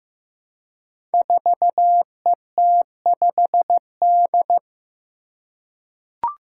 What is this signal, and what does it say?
Morse code, heard as a keyed tone, about 15 wpm, 700 Hz. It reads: 4ET5D
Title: Five Challenging Letter-Number Combinations 15wpm